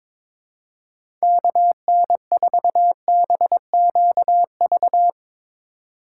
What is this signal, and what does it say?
Morse code, heard as a keyed tone, about 21 wpm, 700 Hz. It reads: KN4BQV
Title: USA Call Signs - 3x Character Spacing - 22wpm